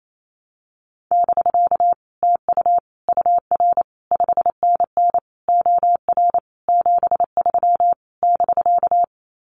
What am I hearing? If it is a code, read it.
Morse code, heard as a keyed tone, about 28 wpm, 700 Hz. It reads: <BK> TU UR 5NN OR 73 <BK>
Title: Common QSO Phrases Encore 28wpm